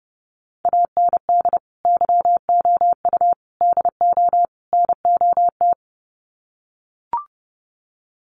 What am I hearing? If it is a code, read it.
Morse code, heard as a keyed tone, about 30 wpm, 700 Hz. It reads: AND YOU DO NOT
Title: Sets of 4 Words 30wpm